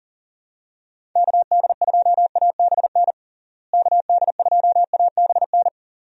Morse code, heard as a keyed tone, about 40 wpm, 700 Hz. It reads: KD2ABN KD2ABN
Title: USA Call Signs - 6 Characters - Repeated 2x - 40wpm